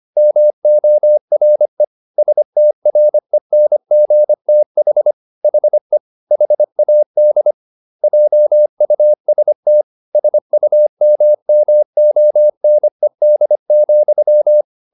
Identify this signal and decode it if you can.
Morse code, heard as a keyed tone, about 25 wpm, 600 Hz. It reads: MORE STRENGTH HE HAD JUST SUMMONED,